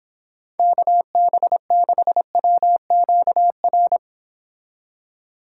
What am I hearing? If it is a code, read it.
Morse code, heard as a keyed tone, about 29 wpm, 700 Hz. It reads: KB6WQR